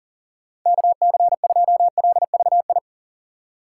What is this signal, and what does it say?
Morse code, heard as a keyed tone, about 40 wpm, 700 Hz. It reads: KC2LVI
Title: USA Call Signs - 6x Character Spacing - 40wpm